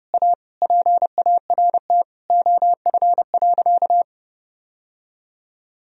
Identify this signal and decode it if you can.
Morse code, heard as a keyed tone, about 30 wpm, 700 Hz. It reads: A PART OF.
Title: Sentences from Top 1000 Words - Review Part 1 - 30wpm